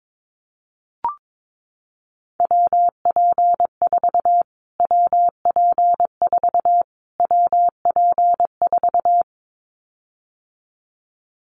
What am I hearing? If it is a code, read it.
Morse code, heard as a keyed tone, about 22 wpm, 700 Hz. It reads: WP4 WP4 WP4
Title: Callsign Prefixes for Top 25 DXCC - Repeated 3x - 22wpm